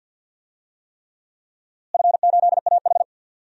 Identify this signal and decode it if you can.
Morse code, heard as a keyed tone, about 51 wpm, 700 Hz. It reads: F8AH
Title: DXCC-100 Call Signs - 4 Characters - Repeated 3x - 50wpm